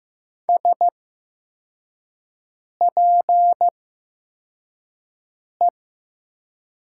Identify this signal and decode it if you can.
Morse code, heard as a keyed tone, about 15 wpm, spaced as for 4 wpm, 700 Hz. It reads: SPE
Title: 8x Character Spacing - Top 500 Words - 15wpm